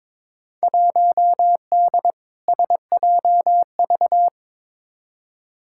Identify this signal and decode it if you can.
Morse code, heard as a keyed tone, about 22 wpm, 700 Hz. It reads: 1D SJV
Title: Field Day Exchanges Encore - 22wpm